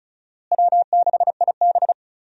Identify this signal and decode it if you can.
Morse code, heard as a keyed tone, about 36 wpm, 700 Hz. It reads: W6IB
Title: USA Call Signs - 4 Characters - Repeated 3x - 35wpm